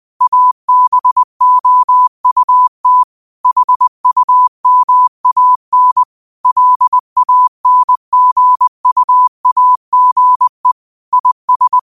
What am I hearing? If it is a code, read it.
Morse code, heard as a keyed tone, about 20 wpm, 1000 Hz. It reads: ABOUT HUMAN LANGUAGE IS